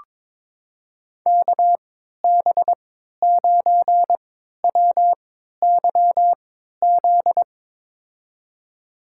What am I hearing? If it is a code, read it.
Morse code, heard as a keyed tone, about 22 wpm, spaced as for 12 wpm, 700 Hz. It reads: KB9WYZ
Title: USA Call Signs - 6 Characters - 3x Character Spacing - 22wpm